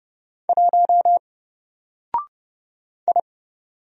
Morse code, heard as a keyed tone, about 30 wpm, 700 Hz. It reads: 1 I
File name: Single-Letter-Number-30wpm